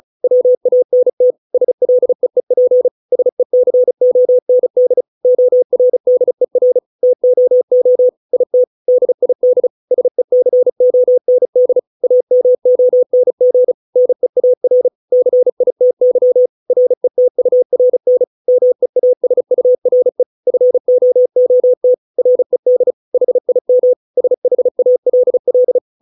Morse code, heard as a keyed tone, about 34 wpm, 500 Hz. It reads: WANT SLEEP SECOND ORDER TOO IT DID SECOND AMONG NEAR CITY RETURN MEASURE FOOT RED HIM SHALL
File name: 632 words / 632 Words 35WPM